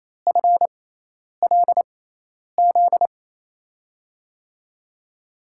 Morse code, heard as a keyed tone, about 27 wpm, spaced as for 10 wpm, 700 Hz. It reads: FLZ
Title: USA Call Signs - 6x Character Spacing - 28wpm